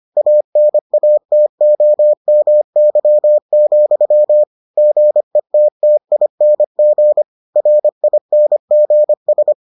Morse code, heard as a keyed tone, about 25 wpm, 600 Hz. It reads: ANATOMY, GETTING RINGS